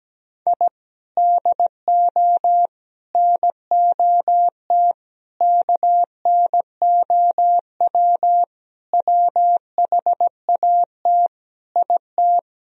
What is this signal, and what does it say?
Morse code, heard as a keyed tone, about 17 wpm, 700 Hz. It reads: I DO NOT KNOW WHAT IT
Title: Sentences from Top 1000 Words - Review Part 2 - 17wpm